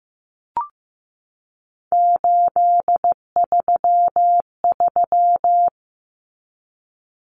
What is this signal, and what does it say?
Morse code, heard as a keyed tone, about 15 wpm, 700 Hz. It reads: 833
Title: Common Cut Numbers - 3-Digit 15wpm